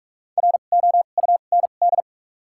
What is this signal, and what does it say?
Morse code, heard as a keyed tone, about 43 wpm, spaced as for 31 wpm, 700 Hz. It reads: ROUND
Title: 2x Character Spacing - Top 500 Words - 45wpm